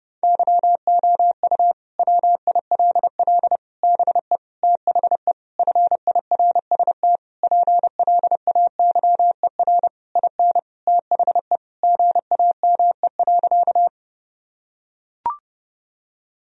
Morse code, heard as a keyed tone, about 30 wpm, 700 Hz. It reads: YOU WILL BE THE FIRST PLAYER IN THE GAME.
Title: Sentences from Top 1000 Words - Review Part 4 - 30wpm